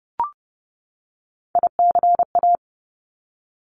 Morse code, heard as a keyed tone, about 30 wpm, 700 Hz. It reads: ICA
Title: Top Three Letter Combinations in Words 30wpm